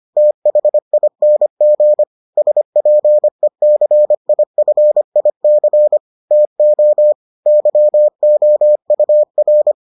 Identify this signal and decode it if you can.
Morse code, heard as a keyed tone, about 25 wpm, 600 Hz. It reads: THING SPECIFIC TO YOUR